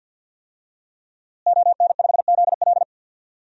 Morse code, heard as a keyed tone, about 51 wpm, 700 Hz. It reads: ON5ZL
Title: DXCC-100 Call Signs - 5 Characters - Repeated 2x - 50wpm